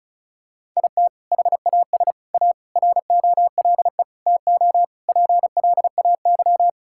Morse code, heard as a keyed tone, about 36 wpm, 700 Hz. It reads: IT HAS A ROLE TO PLAY
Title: Sentences from Top 1000 Words - Review Part 6 - 35wpm